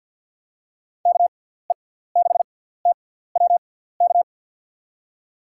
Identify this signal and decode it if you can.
Morse code, heard as a keyed tone, about 47 wpm, spaced as for 17 wpm, 700 Hz. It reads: KE6TWK
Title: USA Call Signs - 6 Characters - 6x Character Spacing - 50wpm